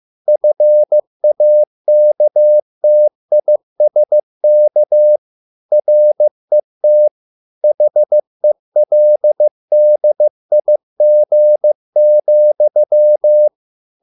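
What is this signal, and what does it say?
Morse code, heard as a keyed tone, about 15 wpm, 600 Hz. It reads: FAKTISK RET HELDIG,
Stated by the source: Morse code at a slow 15 wpm from CW med Gnister.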